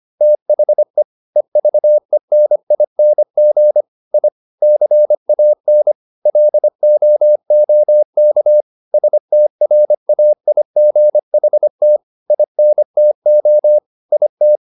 Morse code, heard as a keyed tone, about 25 wpm, 600 Hz. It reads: THE EVENING I CAN LOOK STRAIGHT INTO IT
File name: the old man and the sea part 1 25wpm